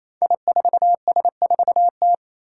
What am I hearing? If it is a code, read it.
Morse code, heard as a keyed tone, about 29 wpm, 700 Hz. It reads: I4S4T
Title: Five Challenging Letter-Number Combinations 28wpm